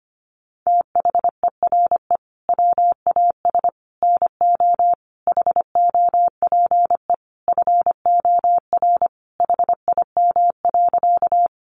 Morse code, heard as a keyed tone, about 25 wpm, 700 Hz. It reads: THERE WAS NO HOPE FOR HIM.
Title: Sentences from Top 1000 Words 25wpm